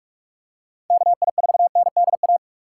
Morse code, heard as a keyed tone, about 46 wpm, 700 Hz. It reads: KI4NDA
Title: USA Call Signs - 6 Characters - 2x Character Spacing - 45wpm